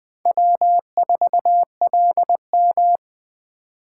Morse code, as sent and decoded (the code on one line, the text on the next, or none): .-- ....- .-.. --
W4LM